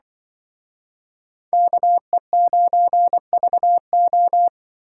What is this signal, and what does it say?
Morse code, heard as a keyed tone, about 25 wpm, 700 Hz. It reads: KE9VO
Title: Speed Racing - USA 2x2 Call Signs 20wpm